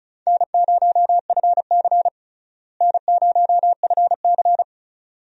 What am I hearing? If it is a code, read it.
Morse code, heard as a keyed tone, about 34 wpm, 700 Hz. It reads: N0FC N0FC
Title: USA Call Signs - 4 Characters - Repeated 2x - 35wpm